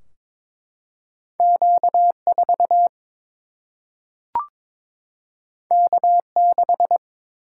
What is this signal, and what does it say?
Morse code, heard as a keyed tone, about 22 wpm, 700 Hz. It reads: Q4 K6
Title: Single-Letter Single-Number 22wpm